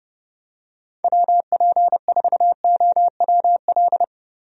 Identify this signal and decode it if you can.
Morse code, heard as a keyed tone, about 31 wpm, 700 Hz. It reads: WP4OWL
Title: USA Call Signs - 6 Characters - Repeated 3x - 30wpm